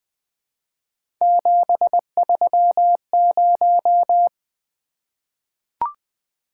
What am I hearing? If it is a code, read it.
Morse code, heard as a keyed tone, about 20 wpm, 700 Hz. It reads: 730
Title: Three Letter-Number Combinations - 20wpm